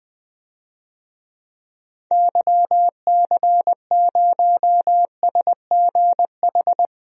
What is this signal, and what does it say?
Morse code, heard as a keyed tone, about 20 wpm, 700 Hz. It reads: YC0SGH
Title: DXCC-100 Call Signs - 6 Characters - Repeated 3x - 20wpm